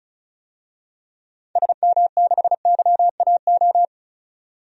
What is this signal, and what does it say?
Morse code, heard as a keyed tone, about 36 wpm, 700 Hz. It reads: SM6YAO
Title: DXCC-100 Call Signs - 6 Characters - 35wpm